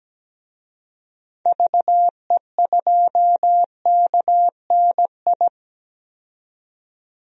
Morse code, heard as a keyed tone, about 17 wpm, 700 Hz. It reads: VE2KNI
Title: DXCC-100 Call Signs - 6 Characters - 17wpm